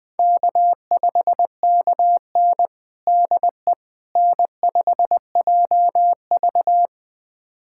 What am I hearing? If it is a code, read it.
Morse code, heard as a keyed tone, about 20 wpm, 700 Hz. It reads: K5KN DE N5JV
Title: USA Callsign DE Callsign - 4 Characters - 20wpm